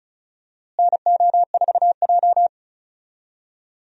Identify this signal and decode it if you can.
Morse code, heard as a keyed tone, about 36 wpm, 700 Hz. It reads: NO4J
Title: USA Call Signs - 4 Characters - Repeated 3x - 35wpm